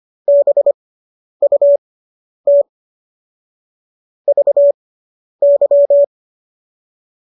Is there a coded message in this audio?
Yes: Morse code from 0.3 to 6.0 s, heard as a keyed tone, about 25 wpm, spaced as for 10 wpm, 570 Hz. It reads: BUT VY